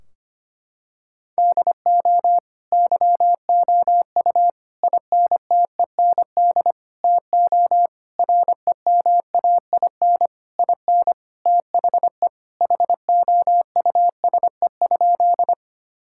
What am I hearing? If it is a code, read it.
Morse code, heard as a keyed tone, about 25 wpm, 700 Hz. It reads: DO YOU INTEND TO REMAIN IN THE HOUSE?
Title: Sentences from Top 1600 Words 25wpm